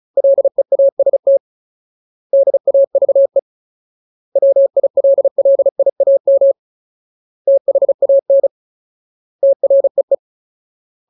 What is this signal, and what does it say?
Morse code, heard as a keyed tone, about 35 wpm, 550 Hz. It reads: LEAST DAVE WILLIAM THAN TREE
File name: Ultimate Word List / 35WPM Ultimate